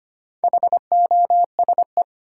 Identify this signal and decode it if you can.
Morse code, heard as a keyed tone, about 25 wpm, 700 Hz. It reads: HOSE
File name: practice / 25wpm / Easy Words with Hints - 25wpm